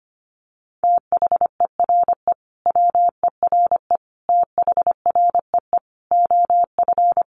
Morse code, heard as a keyed tone, about 25 wpm, 700 Hz. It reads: THERE WERE THREE OF
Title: Sentences from Top 800 Words 25wpm